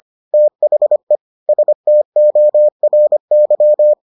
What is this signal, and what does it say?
Morse code, heard as a keyed tone, about 25 wpm, 600 Hz. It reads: THE STORY